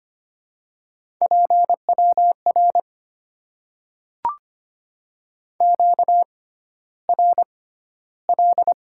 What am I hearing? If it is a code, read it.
Morse code, heard as a keyed tone, about 25 wpm, spaced as for 13 wpm, 700 Hz. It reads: PWR QRL
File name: Top 90 QSO Elements - 6x Character Spacing - 25wpm